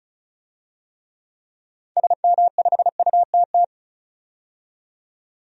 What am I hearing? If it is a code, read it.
Morse code, heard as a keyed tone, about 34 wpm, 700 Hz. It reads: SM5UTT
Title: DXCC-100 Call Signs - 6 Characters - 35wpm